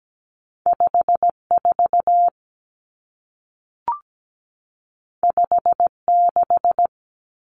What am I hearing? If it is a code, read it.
Morse code, heard as a keyed tone, about 17 wpm, 700 Hz. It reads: 54 56
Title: Common Cut Numbers - 2-Digit 17wpm